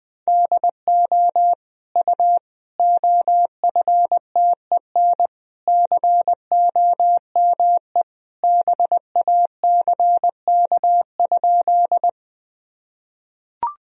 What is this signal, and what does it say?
Morse code, heard as a keyed tone, about 20 wpm, 700 Hz. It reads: DO U OFTEN COME BACK?
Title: Sentences from Top 300 Words and Abbreviations 20wpm